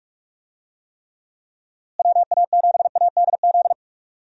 Morse code, heard as a keyed tone, about 46 wpm, 700 Hz. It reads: WA7ADZ
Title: USA Call Signs - 6 Characters - Repeated 2x - 45wpm